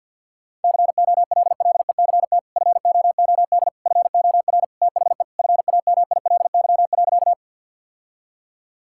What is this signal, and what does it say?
Morse code, heard as a keyed tone, about 52 wpm, 700 Hz. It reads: COLLECT FOOD FOR THE FAMILY.
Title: Sentences from Top 1600 Words 50wpm